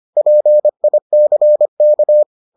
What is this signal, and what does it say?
Morse code, heard as a keyed tone, about 25 wpm, 600 Hz. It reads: PICK